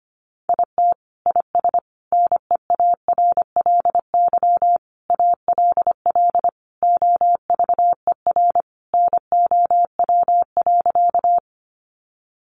Morse code, heard as a keyed tone, about 25 wpm, 700 Hz. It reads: IT IS NEARLY ALL OVER NOW.
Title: Sentences from Top 1300 Words 25wpm